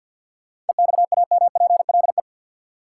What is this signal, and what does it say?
Morse code, heard as a keyed tone, about 51 wpm, 700 Hz. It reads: EXAMPLE
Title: Sentences from Top 400 Words - New Word Drill - 50wpm